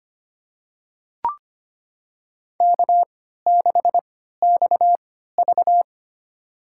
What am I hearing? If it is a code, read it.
Morse code, heard as a keyed tone, about 25 wpm, spaced as for 14 wpm, 700 Hz. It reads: K6XV
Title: USA Call Signs - 4 Characters - 3x Character Spacing - 25wpm